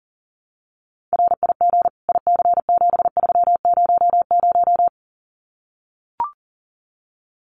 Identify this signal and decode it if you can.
Morse code, heard as a keyed tone, about 39 wpm, 700 Hz. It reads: RIG IC7300
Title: Common QSO Phrases Encore 40wpm